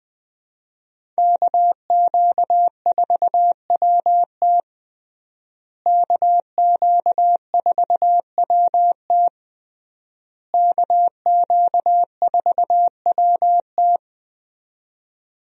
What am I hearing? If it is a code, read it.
Morse code, heard as a keyed tone, about 20 wpm, 700 Hz. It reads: KQ4WT KQ4WT KQ4WT
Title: USA Call Signs - 5 Characters - Repeated 3x - 20wpm